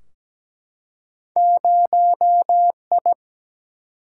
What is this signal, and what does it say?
Morse code, heard as a keyed tone, about 17 wpm, 700 Hz. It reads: 0I
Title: Single-Letter Single-Number 17wpm